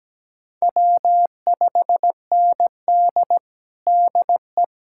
Morse code, heard as a keyed tone, about 17 wpm, 700 Hz. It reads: W5ND DE
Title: USA Callsign DE Callsign - 4 Characters - 17wpm